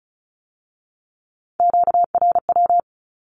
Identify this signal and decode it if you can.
Morse code, heard as a keyed tone, about 35 wpm, 700 Hz. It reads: QRW
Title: Q-Codes with Spoken Meaning - 35wpm